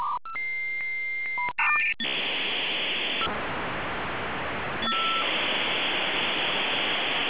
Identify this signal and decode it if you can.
Dial-up modem handshake.